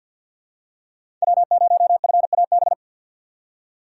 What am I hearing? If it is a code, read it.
Morse code, heard as a keyed tone, about 46 wpm, 700 Hz. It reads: W0FAB
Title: USA Call Signs - 5 Characters - 2x Character Spacing - 50wpm